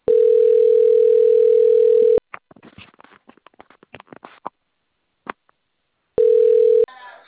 Ringback tone (North America), 440 + 480 Hz.